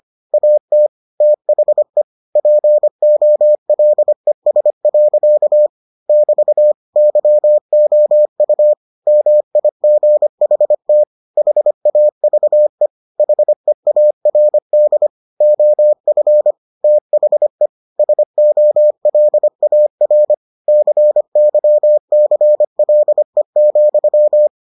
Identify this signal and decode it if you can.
Morse code, heard as a keyed tone, about 25 wpm, 600 Hz. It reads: AT THE POLES. <BT> YOU MIGHT HAVE HEARD OF THE SOLAR CYCLE,